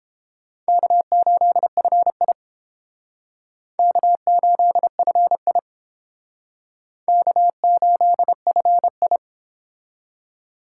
Morse code, heard as a keyed tone, about 31 wpm, 700 Hz. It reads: K8FI K8FI K8FI